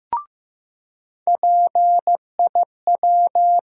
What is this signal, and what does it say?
Morse code, heard as a keyed tone, about 15 wpm, 700 Hz. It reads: PIW
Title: USA Call Sign Postfixes - 3 Letters 15wpm